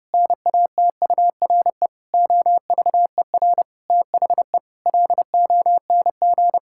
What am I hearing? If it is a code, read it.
Morse code, heard as a keyed tone, about 32 wpm, 700 Hz. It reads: NATURE OVER THE LONG
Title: Sentences from Top 1000 Words - Review Part 6 - 30wpm